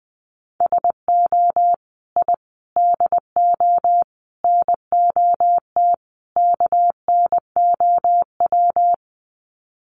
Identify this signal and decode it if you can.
Morse code, heard as a keyed tone, about 20 wpm, 700 Hz. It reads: SO I DO NOT KNOW
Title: Sets of 5 Words Encore 20wpm